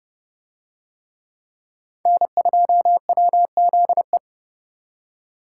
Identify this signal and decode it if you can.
Morse code, heard as a keyed tone, about 27 wpm, 700 Hz. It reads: N2WZE